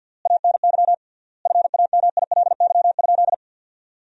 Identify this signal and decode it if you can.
Morse code, heard as a keyed tone, about 52 wpm, 700 Hz. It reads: ANY FAMILY?
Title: 3x Word-Spacing - Sentences from Top 400 Words - 50wpm